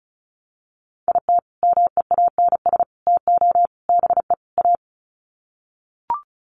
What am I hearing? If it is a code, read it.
Morse code, heard as a keyed tone, about 34 wpm, 700 Hz. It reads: IT MEANS TO BE A
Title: Sets of 5 Words Encore2 35wpm